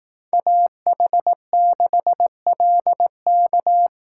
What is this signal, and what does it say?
Morse code, heard as a keyed tone, about 18 wpm, 700 Hz. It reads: AH6LK
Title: Speed Racing - USA 2x2 Call Signs 15wpm